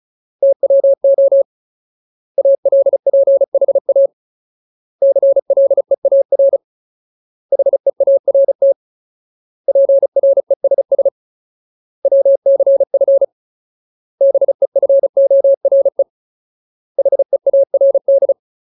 Morse code, heard as a keyed tone, about 35 wpm, 550 Hz. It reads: TWO ALPHA CLEAR HEART PRESS WCF BEFORE HEARD
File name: Ultimate Word List / 35WPM Ultimate